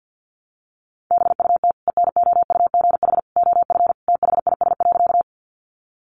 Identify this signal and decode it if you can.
Morse code, heard as a keyed tone, about 50 wpm, 700 Hz. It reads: BUT ENOUGH OF THIS.
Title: Sentences from Top 400 Words 50wpm